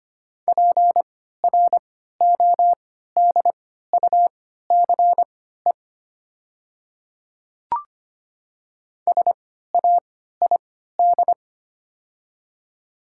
Morse code, heard as a keyed tone, about 25 wpm, spaced as for 14 wpm, 700 Hz. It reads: PRODUCE SAID
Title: 3x Character Spacing - Top 500 Words - 25wpm